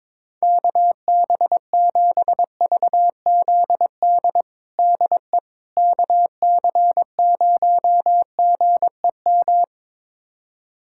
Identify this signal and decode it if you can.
Morse code, heard as a keyed tone, about 21 wpm, 700 Hz. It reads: KB7VZD DE KC0GEM
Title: USA Callsign DE Callsign 22wpm